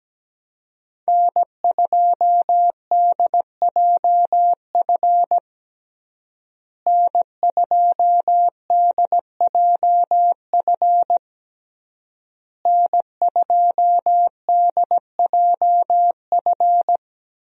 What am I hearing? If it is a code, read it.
Morse code, heard as a keyed tone, about 17 wpm, 700 Hz. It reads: N2DJF N2DJF N2DJF
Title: USA Call Signs - 5 Characters - Repeated 3x - 17wpm